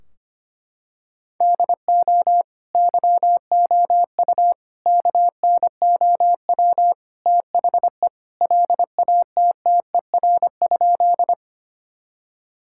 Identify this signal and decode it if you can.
Morse code, heard as a keyed tone, about 25 wpm, 700 Hz. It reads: DO YOU KNOW THE LATTER?
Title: Sentences from Top 1600 Words 25wpm